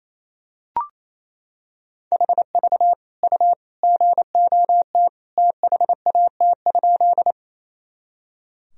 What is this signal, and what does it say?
Morse code, heard as a keyed tone, about 28 wpm, 700 Hz. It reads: HV U GOT THAT?
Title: Sentences from Top 300 Words and Abbreviations 28wpm